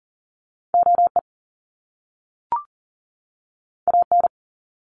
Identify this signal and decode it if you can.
Morse code, heard as a keyed tone, about 40 wpm, 700 Hz. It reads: OE AN
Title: Top Two Letter Combinations in Words 40wpm